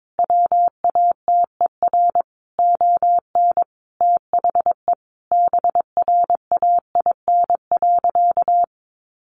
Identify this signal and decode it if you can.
Morse code, heard as a keyed tone, about 22 wpm, 700 Hz. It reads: WATER ON THE BRAIN.
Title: Sentences from Top 1100 Words 22wpm